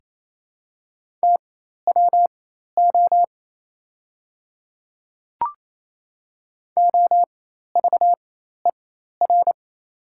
Morse code, heard as a keyed tone, about 28 wpm, spaced as for 13 wpm, 700 Hz. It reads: TWO OVER
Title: 4x Character Spacing - Top 200 Words - 28wpm